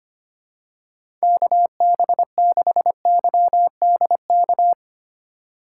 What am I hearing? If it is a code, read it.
Morse code, heard as a keyed tone, about 25 wpm, 700 Hz. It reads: KB6YDK
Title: USA Call Signs - 6 Characters - 3x Character Spacing - 25wpm